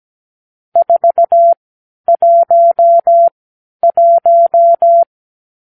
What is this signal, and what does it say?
Morse code, heard as a keyed tone, about 17 wpm, spaced as for 10 wpm, 680 Hz. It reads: 411
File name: k7qo / Lesson 50 (1-6)